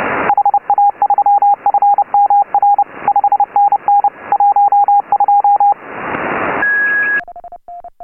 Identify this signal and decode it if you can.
Morse code, heard as a keyed tone, about 29 wpm, 810 Hz, with radio noise. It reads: HA3FMR 5NN 12